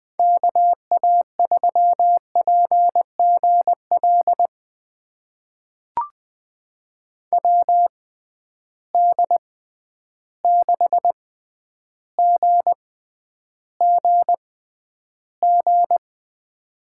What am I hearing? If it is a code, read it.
Morse code, heard as a keyed tone, about 20 wpm, spaced as for 10 wpm, 700 Hz. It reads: KA3PGL WD6GGG
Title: USA Call Signs - 6 Characters - 6x Character Spacing - 20wpm